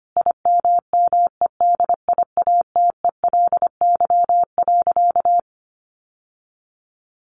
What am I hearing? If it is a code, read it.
Morse code, heard as a keyed tone, about 25 wpm, 700 Hz. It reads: IMMEDIATELY.
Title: Sentences from Top 1100 Words 25wpm